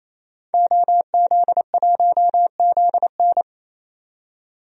Morse code, heard as a keyed tone, about 28 wpm, 700 Hz. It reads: OZ1ZN
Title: DXCC-100 Call Signs - 5 Characters - 28wpm